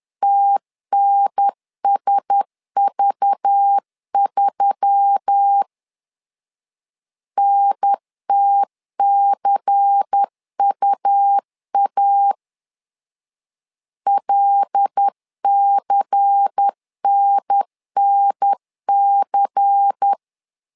Morse code, heard as a keyed tone, about 10 wpm, 800 Hz. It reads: TNSV3 NTCUA LCNNC
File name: LEZIONI / 5 LBVC / 5 LBVC 10wpm 2349